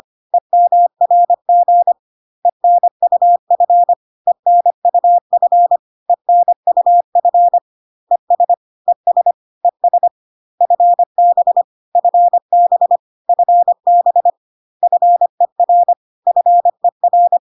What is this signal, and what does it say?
Morse code, heard as a keyed tone, about 25 wpm, 700 Hz. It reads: EMRG ENUF ENUF ENUF ES ES ES FB FB FB FER FER
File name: CW abbreviations alphabetical repeated 3 times 25 25wpm